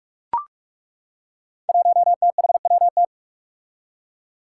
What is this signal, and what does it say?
Morse code, heard as a keyed tone, about 41 wpm, 700 Hz. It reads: 1THWT
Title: Five Challenging Letter-Number Combinations 45wpm